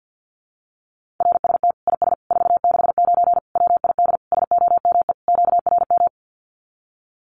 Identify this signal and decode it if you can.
Morse code, heard as a keyed tone, about 50 wpm, 700 Hz. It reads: RST IS 469 WID SOME QRM
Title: Common QSO Phrases 50wpm